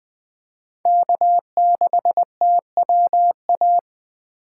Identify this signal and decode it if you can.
Morse code, heard as a keyed tone, about 20 wpm, 700 Hz. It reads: K6TWA
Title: USA Call Signs - 2x Character Spacing - 20wpm